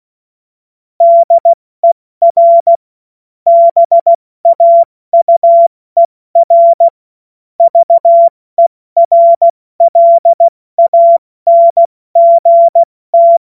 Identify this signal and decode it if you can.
Morse code, heard as a keyed tone, about 16 wpm, 680 Hz. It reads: DER BAUER VERLANGT
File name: Morsekurs / Witze / Witz 10 Juni Mai 2025 70 F80